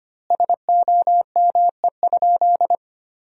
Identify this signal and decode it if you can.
Morse code, heard as a keyed tone, about 25 wpm, 700 Hz. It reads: SOME?
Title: 4x Word-Spacing - Sentences from Top 100 Words - 25wpm